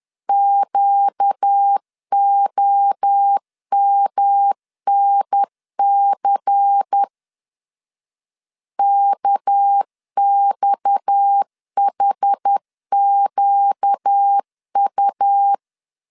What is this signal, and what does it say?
Morse code, heard as a keyed tone, about 10 wpm, 800 Hz. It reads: QOMNC KXHQU
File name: CW / DET / 1 6 1500C S A 10wpm 4141